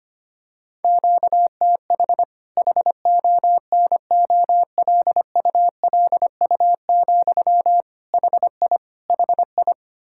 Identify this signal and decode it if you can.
Morse code, heard as a keyed tone, about 25 wpm, 700 Hz. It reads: QTH HONOLULU, HI HI
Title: Practice QSO - Exchange 1 - Beginning - 25wpm